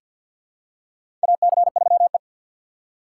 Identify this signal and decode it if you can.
Morse code, heard as a keyed tone, about 53 wpm, 700 Hz. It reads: AC3E
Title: USA Call Signs - 4 Characters - 3x Character Spacing - 50wpm